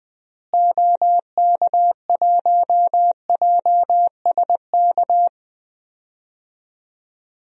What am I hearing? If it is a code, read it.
Morse code, heard as a keyed tone, about 20 wpm, 700 Hz. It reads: OK1JSK
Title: DXCC-100 Call Signs - 6 Characters - Repeated 2x - 20wpm